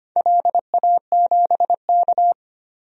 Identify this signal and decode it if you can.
Morse code, heard as a keyed tone, about 25 wpm, 700 Hz. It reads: LA7K
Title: DXCC-100 Call Signs - 4 Characters - 25wpm